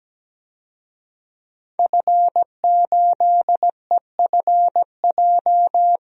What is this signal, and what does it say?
Morse code, heard as a keyed tone, about 17 wpm, 700 Hz. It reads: F8EFJ
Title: DXCC-100 Call Signs - 5 Characters - 17wpm